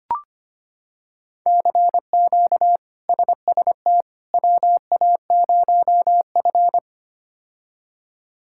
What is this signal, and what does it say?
Morse code, heard as a keyed tone, about 25 wpm, 700 Hz. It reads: CQ SST WA0F
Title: K1USN SST QSO Practice - 25wpm